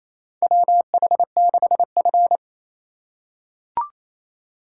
Morse code, heard as a keyed tone, about 28 wpm, 700 Hz. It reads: WH6F
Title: USA Call Signs - 4 Characters - 2x Character Spacing - 28wpm